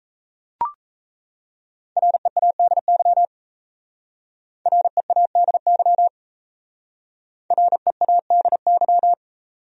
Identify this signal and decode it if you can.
Morse code, heard as a keyed tone, about 35 wpm, 700 Hz. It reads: READY READY READY
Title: Speed-Racing Top 500 Words 28wpm